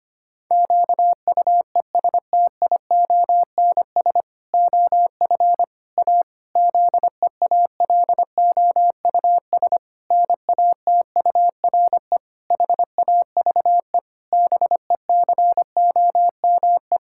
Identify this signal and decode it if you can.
Morse code, heard as a keyed tone, about 25 wpm, 700 Hz. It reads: QUESTIONS OF A ZEALOUS NATURE HAVE BECOME
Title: Pangram Challenge v2 - 25wpm